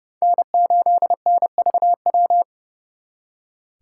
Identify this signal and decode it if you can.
Morse code, heard as a keyed tone, about 38 wpm, 700 Hz. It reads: N8NVW